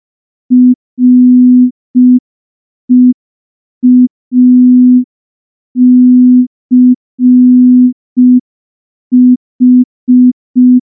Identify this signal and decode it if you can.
Morse code, heard as a keyed tone, about 5 wpm, 250 Hz. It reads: REACH
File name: morse / morse-training-05WPM